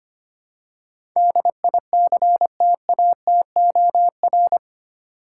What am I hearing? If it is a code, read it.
Morse code, heard as a keyed tone, about 25 wpm, 700 Hz. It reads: DICTATOR
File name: Eight Letter Words - 25wpm